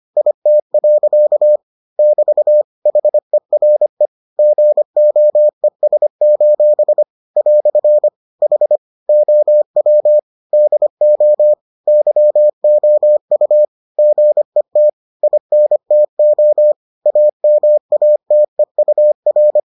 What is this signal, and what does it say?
Morse code, heard as a keyed tone, about 25 wpm, 600 Hz. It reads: IT. <BT> HERE GOES: "HOW DO YOU GET INTO AMATEUR